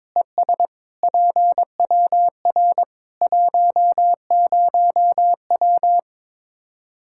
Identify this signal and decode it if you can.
Morse code, heard as a keyed tone, about 22 wpm, 700 Hz. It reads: ES PWR 10W
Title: Practice QSO - Exchange 2 - Info - 22wpm